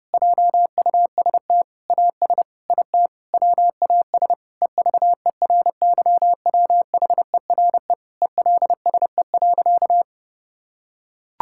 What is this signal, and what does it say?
Morse code, heard as a keyed tone, about 30 wpm, 700 Hz. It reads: JUST AS IT WAS EVERYWHERE ELSE.
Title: Sentences from Top 1600 Words 30wpm